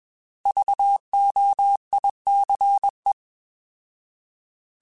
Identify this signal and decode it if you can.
Morse code, heard as a keyed tone, about 21 wpm, 800 Hz. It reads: VOICE
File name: Alphabetical / Alphabetical mp3s / Vs @20